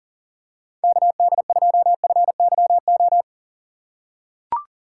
Morse code, heard as a keyed tone, about 43 wpm, 690 Hz. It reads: KD2FYO
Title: USA Call Signs - 4x Character Spacing - 40wpm